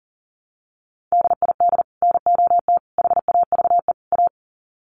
Morse code, heard as a keyed tone, about 40 wpm, 700 Hz. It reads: DID NOT HAVE A
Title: Sets of 4 Words 40wpm